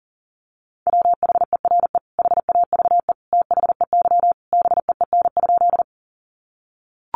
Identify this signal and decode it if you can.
Morse code, heard as a keyed tone, about 40 wpm, 700 Hz. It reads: WHERE HAVE THEY BEEN?
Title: Sentences from Top 200 Words 40wpm